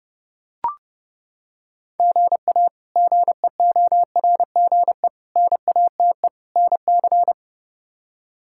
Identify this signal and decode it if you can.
Morse code, heard as a keyed tone, about 32 wpm, 700 Hz. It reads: GA GEORGE NATE NC
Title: K1USN SST QSO Practice - 30wpm